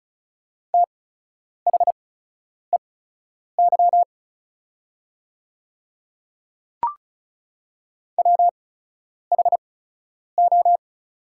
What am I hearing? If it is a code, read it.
Morse code, heard as a keyed tone, about 34 wpm, spaced as for 10 wpm, 690 Hz. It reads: THEY WHO
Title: 8x Character Spacing - Top 100 Words - 35wpm